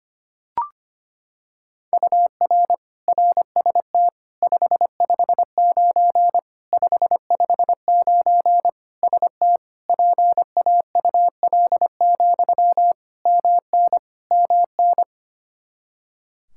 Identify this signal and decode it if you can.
Morse code, heard as a keyed tone, about 25 wpm, 700 Hz. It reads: UR RST 559 559 ST PAUL, MN MN
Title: Practice QSO - Exchange 1 - Beginning - 25wpm